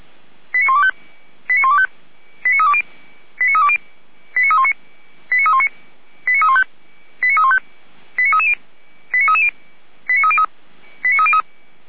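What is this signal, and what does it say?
Five-tone selective call (ZVEI), heard as quick runs of five tones, tones of about 70 ms. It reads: 89125 89125 89320 89320 89319 89319 89315 89315 89330 89330 89303 89303
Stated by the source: Five-tone ZVEI selective call received on SDR.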